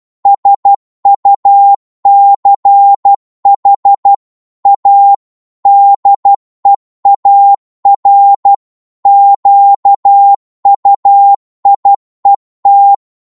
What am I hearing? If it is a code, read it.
Morse code, heard as a keyed tone, about 12 wpm, 800 Hz. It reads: SUCH A DEAR QUIET
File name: Morse / psg-12wpm